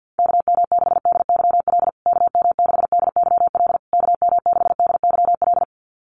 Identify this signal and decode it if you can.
Morse code, heard as a keyed tone, about 48 wpm, 700 Hz. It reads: KM6DYL KM6DYL KM6DYL
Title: USA Call Signs Repeated 3x 50wpm